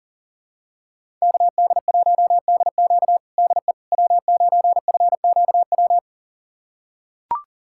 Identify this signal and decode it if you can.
Morse code, heard as a keyed tone, about 40 wpm, 700 Hz. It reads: KD1DQ DE W9FQW
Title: USA Callsign DE Callsign - 5 Characters - 40wpm